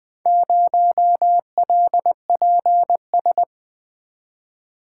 Morse code, heard as a keyed tone, about 20 wpm, 700 Hz. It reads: 0LPS
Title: Four Letter-Number Combinations - 20wpm